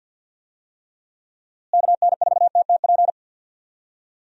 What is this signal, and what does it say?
Morse code, heard as a keyed tone, about 52 wpm, 700 Hz. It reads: KN4TTP
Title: USA Call Signs - 6 Characters - Repeated 2x - 50wpm